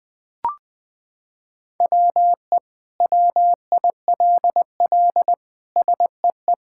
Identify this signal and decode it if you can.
Morse code, heard as a keyed tone, about 21 wpm, 700 Hz. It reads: WE WILL SEE
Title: Sentences with 7-Word Phrases - 20wpm